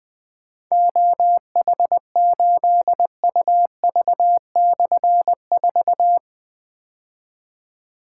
Morse code, heard as a keyed tone, about 20 wpm, 700 Hz. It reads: OH8UV/4
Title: Difficult Call Signs 20wpm